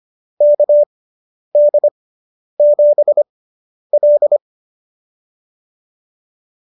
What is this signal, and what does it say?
Morse code, heard as a keyed tone, about 25 wpm, spaced as for 10 wpm, 590 Hz. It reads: KD7L